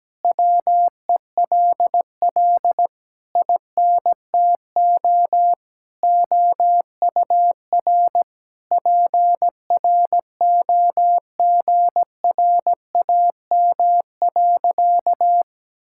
Morse code, heard as a keyed tone, about 17 wpm, 700 Hz. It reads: WELL INTO OUR PROGRAM.
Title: Sentences from Top 1000 Words - Review Part 5 - 17wpm